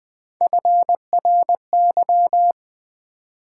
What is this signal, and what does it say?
Morse code, heard as a keyed tone, about 20 wpm, 700 Hz. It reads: FRY
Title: Build A Word - 1 Letter At A Time - 3-Letter Words - 20wpm